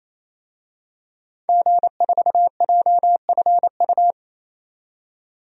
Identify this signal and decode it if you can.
Morse code, heard as a keyed tone, about 28 wpm, 700 Hz. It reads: G4JFU
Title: DXCC-100 Call Signs - 5 Characters - 28wpm